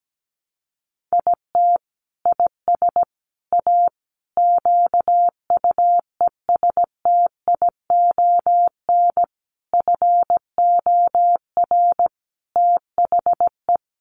Morse code, heard as a keyed tone, about 17 wpm, 700 Hz. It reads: IT IS A QUESTION FOR THE
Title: Sentences from Top 1300 Words 17wpm